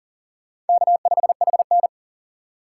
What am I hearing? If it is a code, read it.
Morse code, heard as a keyed tone, about 54 wpm, 700 Hz. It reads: K5HN